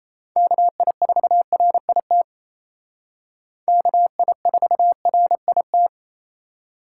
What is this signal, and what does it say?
Morse code, heard as a keyed tone, about 35 wpm, 700 Hz. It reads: KI4RIT KI4RIT
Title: Speed Racing - USA 2x3 Call Signs 28wpm